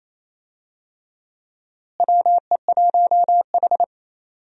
Morse code, heard as a keyed tone, about 27 wpm, 700 Hz. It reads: WE1H
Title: USA Call Signs - 4 Characters - Repeated 3x - 28wpm